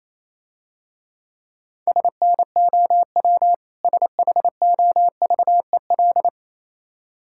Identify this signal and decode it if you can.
Morse code, heard as a keyed tone, about 28 wpm, 700 Hz. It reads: SNOW SHOVEL
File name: Easy Words - Speed-Racing with Hints - 22wpm